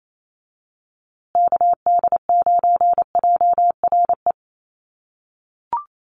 Morse code, heard as a keyed tone, about 28 wpm, 700 Hz. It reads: KD9JRE
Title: USA Call Signs Repeated 2x 28wpm